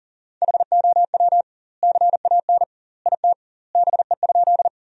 Morse code, heard as a keyed tone, about 40 wpm, 700 Hz. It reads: HOW CAN IT BE?
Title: 2x Word-Spacing - Sentences from Top 100 Words - 40wpm